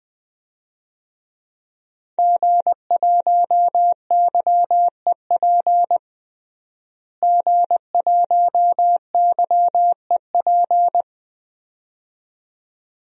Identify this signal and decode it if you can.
Morse code, heard as a keyed tone, about 20 wpm, 700 Hz. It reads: G1YEP G1YEP
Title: DXCC-100 Call Signs - 5 Characters - Repeated 2x - 20wpm